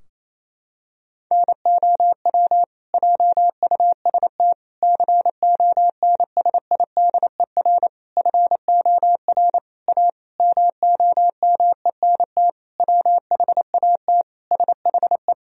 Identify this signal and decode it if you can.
Morse code, heard as a keyed tone, about 28 wpm, 700 Hz. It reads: NOW JUST CONSIDER FOR A MOMENT WHAT SHE
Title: Sentences from Top 1000 Words - Review Part 4 - 28wpm